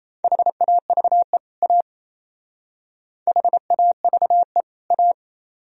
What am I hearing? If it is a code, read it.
Morse code, heard as a keyed tone, about 34 wpm, 700 Hz. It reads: HAVE A HAVE A
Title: Speed-Racing - Sets of 2 Words 28wpm